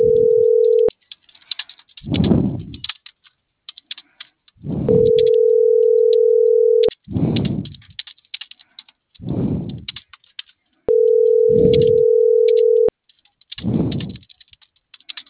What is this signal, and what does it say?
Ringback tone (North America), 440 + 480 Hz.